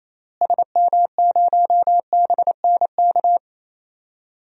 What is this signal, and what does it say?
Morse code, heard as a keyed tone, about 28 wpm, 700 Hz. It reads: SM0BNK
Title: DXCC-100 Call Signs - 6 Characters - Repeated 3x - 28wpm